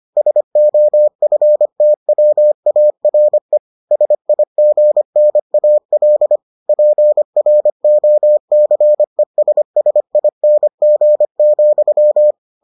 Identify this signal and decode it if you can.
Morse code, heard as a keyed tone, about 25 wpm, 600 Hz. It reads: SOFTWARE SIGNAL PROCESSING,